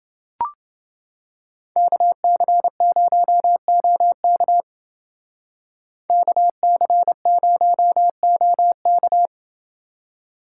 Morse code, heard as a keyed tone, about 29 wpm, 700 Hz. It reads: KC0OK KC0OK